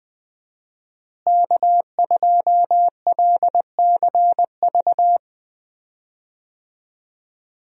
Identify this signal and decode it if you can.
Morse code, heard as a keyed tone, about 20 wpm, 700 Hz. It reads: K2LCV
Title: USA Call Signs - 5 Characters - Repeated 2x - 20wpm